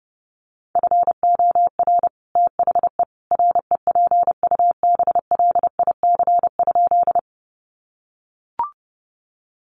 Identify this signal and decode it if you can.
Morse code, heard as a keyed tone, about 30 wpm, 700 Hz. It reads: FOR THE REPUBLIC?
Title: Sentences from Top 1500 Words 30wpm